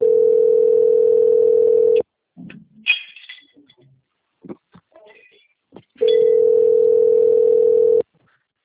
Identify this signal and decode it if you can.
Ringback tone (North America), 440 + 480 Hz.
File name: out- - - - - .9116